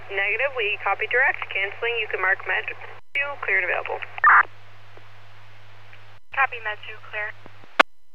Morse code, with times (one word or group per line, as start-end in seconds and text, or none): none